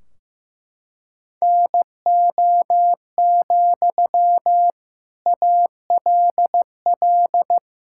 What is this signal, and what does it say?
Morse code, heard as a keyed tone, about 16 wpm, 700 Hz. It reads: NO, ALL
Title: Sentences from Top 1600 Words 15wpm